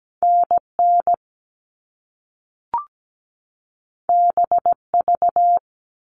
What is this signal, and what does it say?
Morse code, heard as a keyed tone, about 17 wpm, 700 Hz. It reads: NN BV
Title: All Cut Numbers - 2-Digit 17wpm